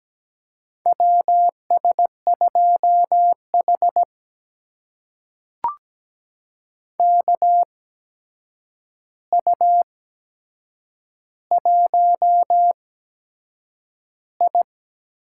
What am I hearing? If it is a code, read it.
Morse code, heard as a keyed tone, about 17 wpm, spaced as for 7 wpm, 700 Hz. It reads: WS2H KU1I
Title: USA Call Signs - 4 Characters - 8x Character Spacing - 17wpm